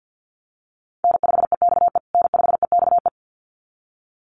Morse code, heard as a keyed tone, about 50 wpm, 700 Hz. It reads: N5EXE N5EXE
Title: USA Call Signs Repeated 2x 50wpm